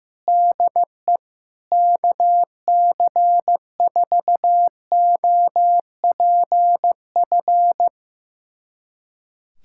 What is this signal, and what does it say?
Morse code, heard as a keyed tone, about 15 wpm, 700 Hz. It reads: DE KC4OPF
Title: USA Callsign DE Callsign 15wpm